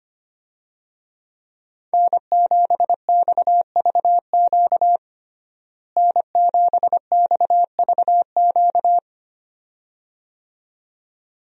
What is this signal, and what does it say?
Morse code, heard as a keyed tone, about 25 wpm, 700 Hz. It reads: N7XVQ N7XVQ
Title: USA Call Signs - 5 Characters - Repeated 2x - 25wpm